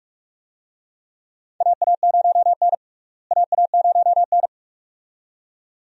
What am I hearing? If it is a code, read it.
Morse code, heard as a keyed tone, about 42 wpm, 700 Hz. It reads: AA0N AA0N
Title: USA Call Signs - 4 Characters - Repeated 2x - 45wpm